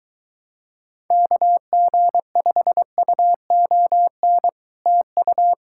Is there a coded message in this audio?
Yes: Morse code from 1.1 to 5.5 s, heard as a keyed tone, about 23 wpm, 700 Hz. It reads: KG5UON TU